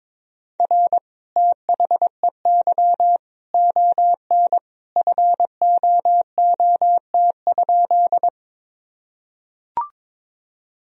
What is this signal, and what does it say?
Morse code, heard as a keyed tone, about 22 wpm, 700 Hz. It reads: R THEY ON FOOT?
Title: Sentences from Top 500 Words and Abbreviations 22wpm